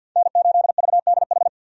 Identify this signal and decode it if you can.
Morse code, heard as a keyed tone, about 49 wpm, 700 Hz. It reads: N8VDH
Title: USA Call Signs - 5 Characters - 50wpm